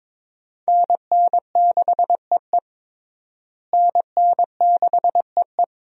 Morse code, heard as a keyed tone, about 22 wpm, 700 Hz. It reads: NN6EE NN6EE
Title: USA Call Signs - 5 Characters - Repeated 2x - 22wpm